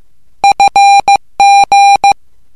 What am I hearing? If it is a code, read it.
Morse code, heard as a keyed tone, about 15 wpm, 800 Hz. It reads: FG
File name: morse / english